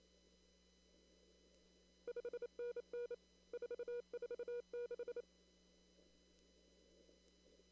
Morse code, heard as a keyed tone, about 28 wpm, 480 Hz. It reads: 5NN 446
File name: CW / kt5j wpx cw-op-k5pi